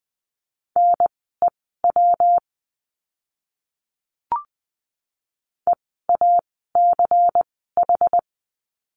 Morse code, heard as a keyed tone, about 20 wpm, spaced as for 15 wpm, 700 Hz. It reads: NEW EACH
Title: 192 Lesson 31f - Letter v - Words 20wpm